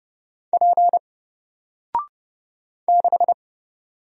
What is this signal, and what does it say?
Morse code, heard as a keyed tone, about 31 wpm, 700 Hz. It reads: P 6
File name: Single-Letter-Number-30wpm